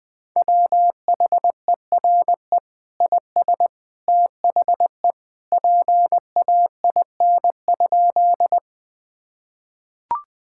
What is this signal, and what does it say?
Morse code, heard as a keyed tone, about 21 wpm, 700 Hz. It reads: WHERE IS THE PAIN?
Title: Sentences from Top 1600 Words 20wpm